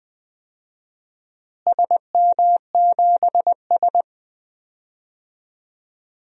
Morse code, heard as a keyed tone, about 21 wpm, 700 Hz. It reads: SM7S
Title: DXCC-100 Call Signs - 4 Characters - Repeated 2x - 20wpm